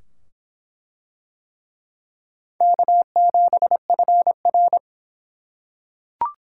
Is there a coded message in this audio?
Yes: Morse code from 2.6 to 4.8 s, heard as a keyed tone, about 22 wpm, 700 Hz. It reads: K7FR